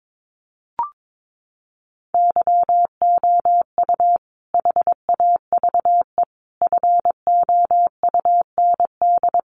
Morse code, heard as a keyed tone, about 22 wpm, 700 Hz. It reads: YOU HAVE FOUND
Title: Sentences from Top 1500 Words 22wpm